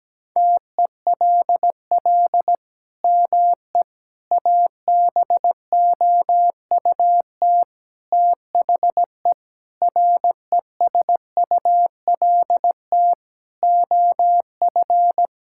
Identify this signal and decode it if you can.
Morse code, heard as a keyed tone, about 17 wpm, 700 Hz. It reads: TELL ME ABOUT THE RESULT OF
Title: Sentences from Top 1000 Words - Review Part 5 - 17wpm